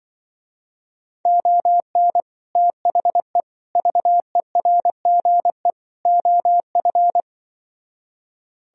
Morse code, heard as a keyed tone, about 24 wpm, 700 Hz. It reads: ON THE VERGE OF
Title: Speed-Racing - Sets of 4 Words 20wpm